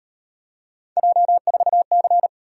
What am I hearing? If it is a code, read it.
Morse code, heard as a keyed tone, about 36 wpm, 700 Hz. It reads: J4C